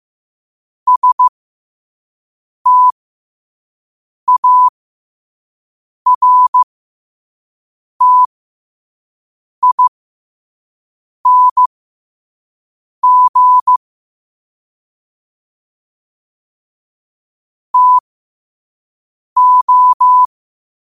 Morse code, heard as a keyed tone, about 15 wpm, spaced as for 5 wpm, 1000 Hz. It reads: STARTING TO
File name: qotdfiles / 5wpm